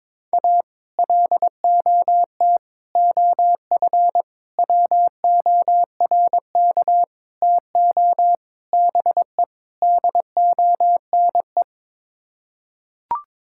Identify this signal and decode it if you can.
Morse code, heard as a keyed tone, about 22 wpm, 700 Hz. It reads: A LOT OF WORK TO BE DONE
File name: practice / 22wpm / Sets of 7 Words - 22wpm